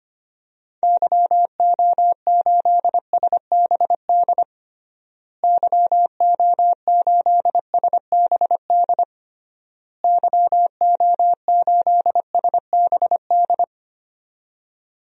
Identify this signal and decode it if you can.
Morse code, heard as a keyed tone, about 25 wpm, 700 Hz. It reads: YO8SBD YO8SBD YO8SBD
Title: DXCC-100 Call Signs - 6 Characters - Repeated 3x - 25wpm